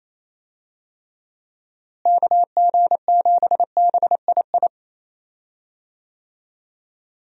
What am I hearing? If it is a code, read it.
Morse code, heard as a keyed tone, about 28 wpm, 700 Hz. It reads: KG7BII
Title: USA Call Signs - 6 Characters - Repeated 2x - 28wpm